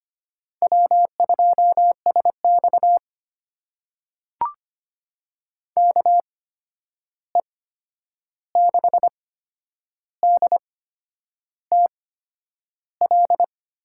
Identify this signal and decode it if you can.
Morse code, heard as a keyed tone, about 25 wpm, spaced as for 9 wpm, 700 Hz. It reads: W2SX KE6DTL
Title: USA Call Signs - 8x Character Spacing - 25wpm